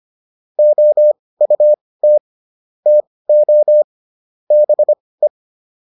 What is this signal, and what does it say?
Morse code, heard as a keyed tone, about 25 wpm, spaced as for 18 wpm, 600 Hz. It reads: OUT TO BE